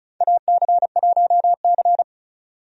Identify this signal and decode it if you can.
Morse code, heard as a keyed tone, about 34 wpm, 700 Hz. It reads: AC1C
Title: USA Call Signs - 4 Characters - 35wpm